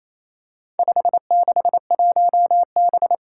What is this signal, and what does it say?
Morse code, heard as a keyed tone, about 28 wpm, 700 Hz. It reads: 561B
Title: Four Challenging Letter-Number Combinations 28wpm